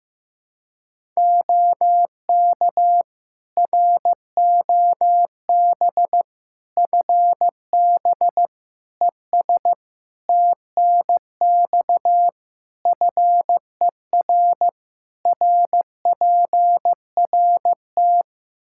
Morse code, heard as a keyed tone, about 16 wpm, 700 Hz. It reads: OK ROB FB ES TNX FER RPRT
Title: Practice QSO - Exchange 2 - Info - 15wpm